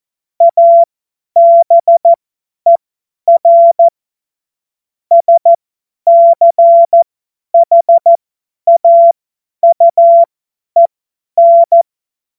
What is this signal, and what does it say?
Morse code, heard as a keyed tone, about 14 wpm, spaced as for 10 wpm, 680 Hz. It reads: ABER SCHAUEN
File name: Morsekurs / Witze / Witz 24.Oktober 2023 50 F70